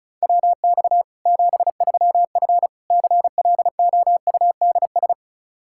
Morse code, heard as a keyed tone, about 34 wpm, 700 Hz. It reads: WX 73F CLOUDS
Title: Practice QSO - Exchange 2 - Info - 35wpm